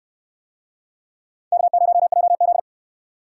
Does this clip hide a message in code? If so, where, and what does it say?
Yes: Morse code from 1.5 to 2.6 s, heard as a keyed tone, about 69 wpm, 700 Hz. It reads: K0JZ